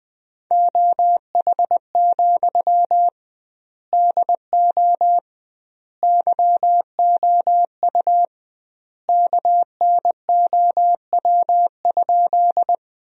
Morse code, heard as a keyed tone, about 20 wpm, 700 Hz. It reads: OH, DO YOU KNOW?
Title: 2x Word-Spacing - Sentences from Top 500 Words - 20wpm